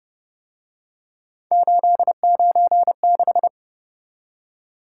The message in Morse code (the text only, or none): 896